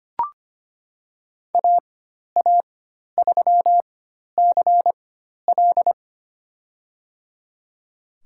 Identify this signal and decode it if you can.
Morse code, heard as a keyed tone, about 25 wpm, spaced as for 12 wpm, 700 Hz. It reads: AA3CL
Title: USA Call Signs - 4x Character Spacing - 25wpm